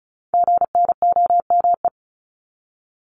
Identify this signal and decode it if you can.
Morse code, heard as a keyed tone, about 35 wpm, 700 Hz. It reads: GNOME
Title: Multi-letter Phonemes 35wpm